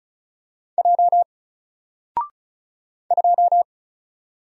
Morse code, heard as a keyed tone, about 35 wpm, 700 Hz. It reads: J 2
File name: Single-Letter-Number-35wpm